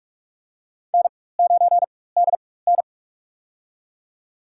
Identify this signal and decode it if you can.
Morse code, heard as a keyed tone, about 46 wpm, spaced as for 21 wpm, 700 Hz. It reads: N9DN
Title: USA Call Signs - 4 Characters - 4x Character Spacing - 45wpm